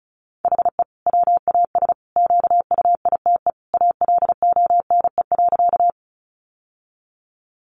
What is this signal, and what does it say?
Morse code, heard as a keyed tone, about 34 wpm, 700 Hz. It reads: HE WAS QUITE ALONE.
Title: Sentences from Top 1000 Words 35wpm